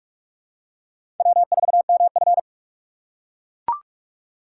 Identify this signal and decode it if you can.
Morse code, heard as a keyed tone, about 43 wpm, 700 Hz. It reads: W4MF
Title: USA Call Signs - 4 Characters - 3x Character Spacing - 45wpm